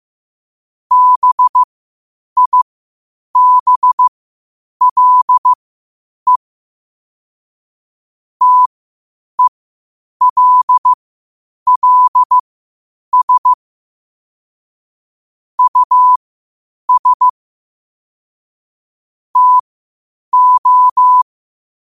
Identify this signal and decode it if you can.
Morse code, heard as a keyed tone, about 15 wpm, spaced as for 8 wpm, 1000 Hz. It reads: BIBLE TELLS US TO
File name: qotdfiles / 8wpm